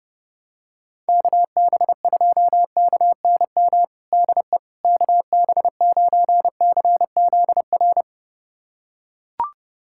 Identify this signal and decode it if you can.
Morse code, heard as a keyed tone, about 32 wpm, 700 Hz. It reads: KB2KNM DE KB9CZR
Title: USA Callsign DE Callsign - 6 Characters - 30wpm